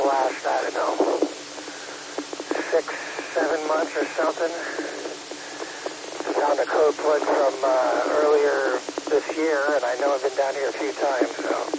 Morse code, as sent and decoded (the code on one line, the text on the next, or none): none